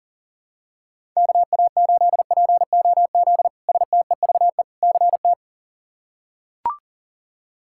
Morse code, heard as a keyed tone, about 37 wpm, 700 Hz. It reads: KA8POZ STEVE CT
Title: K1USN SST QSO Practice - 40wpm